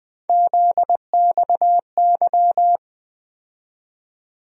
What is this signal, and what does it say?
Morse code, heard as a keyed tone, about 20 wpm, 700 Hz. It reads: ZXY
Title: USA Call Sign Postfixes - 3 Letters 20wpm